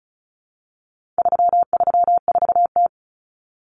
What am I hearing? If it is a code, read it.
Morse code, heard as a keyed tone, about 34 wpm, 700 Hz. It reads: 334T
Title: Common Cut Numbers - 4-Digit 35wpm